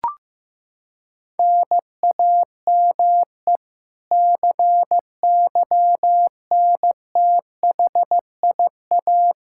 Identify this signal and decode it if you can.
Morse code, heard as a keyed tone, about 15 wpm, 700 Hz. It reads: NAME CYNTHIA
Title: K1USN SST QSO Practice 15wpm